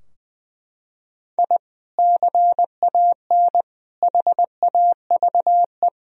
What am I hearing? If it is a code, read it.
Morse code, heard as a keyed tone, about 20 wpm, 700 Hz. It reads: I CAN HAVE
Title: Sentences from Top 1000 Words - Review Part 2 - 20wpm